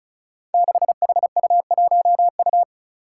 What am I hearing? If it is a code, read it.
Morse code, heard as a keyed tone, about 34 wpm, 700 Hz. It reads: 6HU1U
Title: Five Challenging Letter-Number Combinations 35wpm